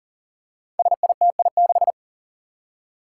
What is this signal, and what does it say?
Morse code, heard as a keyed tone, about 40 wpm, 700 Hz. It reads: SITI6
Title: Five Challenging Letter-Number Combinations 40wpm